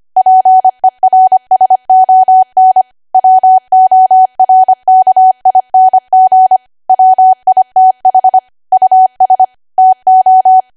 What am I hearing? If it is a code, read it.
Morse code, heard as a keyed tone, about 25 wpm, 750 Hz. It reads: PERSON WORKING WITH US TO
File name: Archive / 25 WPM / 25WPM